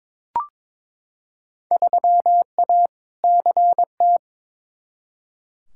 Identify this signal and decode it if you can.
Morse code, heard as a keyed tone, about 21 wpm, 700 Hz. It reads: 3A CT
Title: Field Day Exchanges - 22wpm